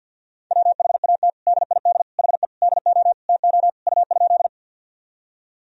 Morse code, heard as a keyed tone, about 53 wpm, 700 Hz. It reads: WHAT DID HE DO TO U?
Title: Sentences from Top 100 Words and Abbreviations 50wpm